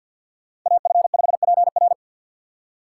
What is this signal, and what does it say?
Morse code, heard as a keyed tone, about 47 wpm, 700 Hz. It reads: AF5PR
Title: USA Call Signs - 5 Characters - 2x Character Spacing - 50wpm